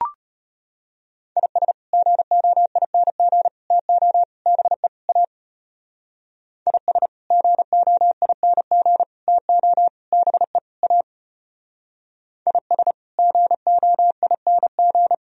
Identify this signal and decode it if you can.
Morse code, heard as a keyed tone, about 30 wpm, 700 Hz. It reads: IS GOING TO BE A IS GOING TO BE A IS GOING
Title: Speed-Racing - Sets of 5 Words 25wpm